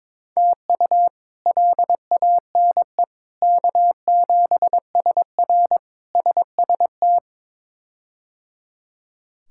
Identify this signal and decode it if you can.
Morse code, heard as a keyed tone, about 22 wpm, 700 Hz. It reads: TU LANE K7SR SST
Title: K1USN SST QSO Practice - 22wpm